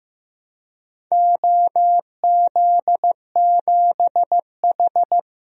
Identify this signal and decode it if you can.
Morse code, heard as a keyed tone, about 14 wpm, 700 Hz. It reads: OZ7H
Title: DXCC-100 Call Signs - 4 Characters - Repeated 2x - 15wpm